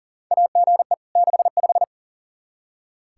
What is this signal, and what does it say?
Morse code, heard as a keyed tone, about 40 wpm, 700 Hz. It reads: AGE 65
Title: K1USN SST QSO Practice 40wpm